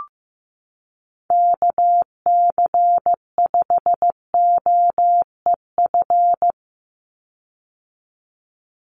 Morse code, heard as a keyed tone, about 15 wpm, 700 Hz. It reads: KC5OEF
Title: USA Call Signs 15wpm